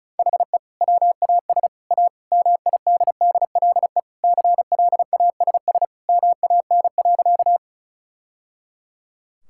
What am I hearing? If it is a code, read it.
Morse code, heard as a keyed tone, about 36 wpm, 700 Hz. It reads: HE WAS A MIDDLE CLASS MAN.
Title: Sentences from Top 1000 Words - Review Part 6 - 35wpm